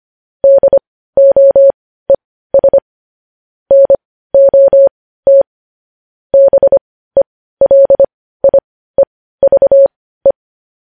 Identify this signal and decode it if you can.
Morse code, heard as a keyed tone, about 25 wpm, spaced as for 15 wpm, 550 Hz. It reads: DOES NOT BELIEVE
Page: News headlines in morse code at 15 wpm